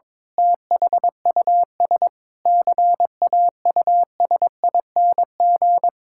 Morse code, heard as a keyed tone, about 22 wpm, 700 Hz. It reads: THUS CAUSING
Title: Pangram Challenge v2 - 22wpm